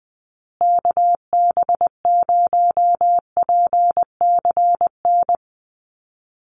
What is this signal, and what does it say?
Morse code, heard as a keyed tone, about 20 wpm, 700 Hz. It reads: KB0PCN
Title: USA Call Signs Repeated 2x 20wpm